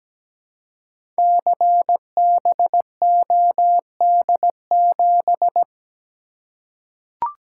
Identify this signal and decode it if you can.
Morse code, heard as a keyed tone, about 17 wpm, 700 Hz. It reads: CBOD7
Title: Code Groups - 5 Letter:Numbers - 17wpm